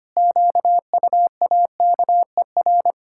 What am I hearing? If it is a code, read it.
Morse code, heard as a keyed tone, about 25 wpm, 700 Hz. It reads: QUAKER
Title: Pangram Challenge v2 - 25wpm